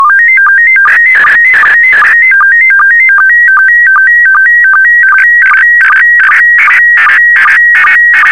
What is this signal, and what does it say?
SSTV (slow-scan television), PD 50.
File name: sstv / PD50-VK4MSL